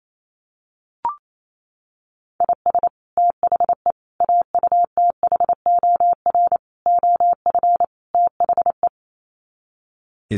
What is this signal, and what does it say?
Morse code, heard as a keyed tone, about 28 wpm, 700 Hz. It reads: IS THE AUTHOR OF THE
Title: Sets of 5 Words Encore 28wpm